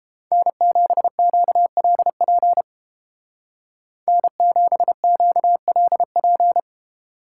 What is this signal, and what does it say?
Morse code, heard as a keyed tone, about 31 wpm, 700 Hz. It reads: N7QLP N7QLP